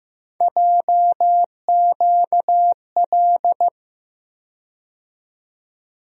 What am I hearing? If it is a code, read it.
Morse code, heard as a keyed tone, about 15 wpm, 700 Hz. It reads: JQL
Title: USA Call Sign Postfixes - 3 Letters 15wpm